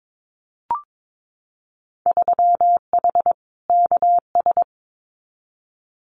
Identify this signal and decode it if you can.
Morse code, heard as a keyed tone, about 22 wpm, 700 Hz. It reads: 3H KS
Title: Winter Field Day Exchanges Encore 22wpm